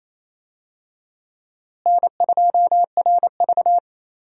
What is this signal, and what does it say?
Morse code, heard as a keyed tone, about 28 wpm, 700 Hz. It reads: N2RV
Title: USA Call Signs - 4 Characters - Repeated 3x - 28wpm